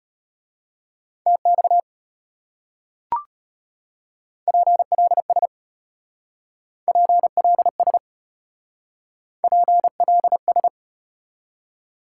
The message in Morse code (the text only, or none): TX PLS PLS PLS